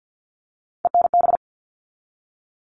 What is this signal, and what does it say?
Morse code, heard as a keyed tone, about 50 wpm, 700 Hz. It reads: ENB
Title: All Cut Numbers - 3-Digit 50wpm